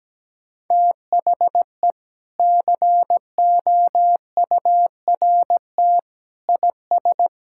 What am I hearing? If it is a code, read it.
Morse code, heard as a keyed tone, about 17 wpm, 700 Hz. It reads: THE COURT IS
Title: Sentences from Top 1600 Words 17wpm